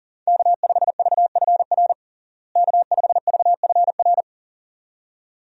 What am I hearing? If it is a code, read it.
Morse code, heard as a keyed tone, about 40 wpm, 700 Hz. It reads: K5VFR K5VFR
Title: USA Call Signs - 5 Characters - Repeated 2x - 40wpm